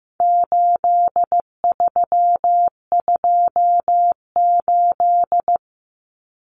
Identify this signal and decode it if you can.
Morse code, heard as a keyed tone, about 15 wpm, 700 Hz. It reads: 8328
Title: Numbers - 4-Digit 15wpm